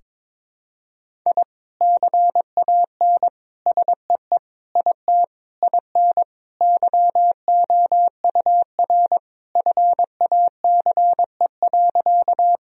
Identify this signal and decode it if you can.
Morse code, heard as a keyed tone, about 22 wpm, 700 Hz. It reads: I CAN SEE IT IN YOUR FACE.
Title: Sentences from Top 1000 Words - Review Part 2 - 22wpm